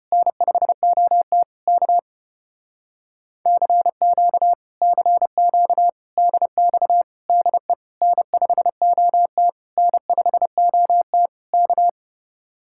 Morse code, heard as a keyed tone, about 30 wpm, 700 Hz. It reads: N5OT K CQ CQ DX DE N5OT N5OT K
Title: Speed-Racing Common QSO Phrases - 25wpm